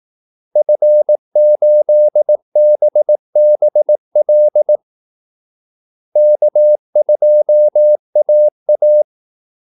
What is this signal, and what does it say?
Morse code, heard as a keyed tone, about 18 wpm, 600 Hz. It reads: F8BBL K2AA
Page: Kaldesignaler 18 wpm